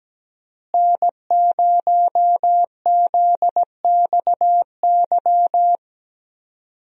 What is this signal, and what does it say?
Morse code, heard as a keyed tone, about 17 wpm, 700 Hz. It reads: N0ZXY
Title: USA Call Signs - 5 Characters - Repeated 3x - 17wpm